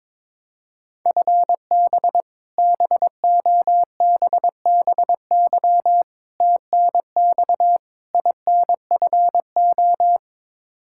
Morse code, heard as a keyed tone, about 22 wpm, 700 Hz. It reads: FB BOBBY TNX INFO
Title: Practice QSO - Exchange 3 - More Info - 22wpm